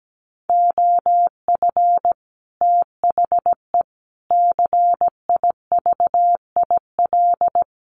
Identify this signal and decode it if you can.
Morse code, heard as a keyed tone, about 17 wpm, 700 Hz. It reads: OF THE CIVIL
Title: Sets of 5 Words Encore2 17wpm